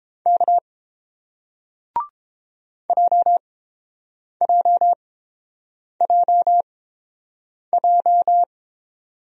Morse code, heard as a keyed tone, about 30 wpm, spaced as for 7 wpm, 700 Hz. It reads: K JJJJ